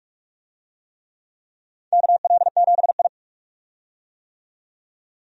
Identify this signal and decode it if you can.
Morse code, heard as a keyed tone, about 47 wpm, 700 Hz. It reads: KL7I
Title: DXCC-100 Call Signs - 4 Characters - 45wpm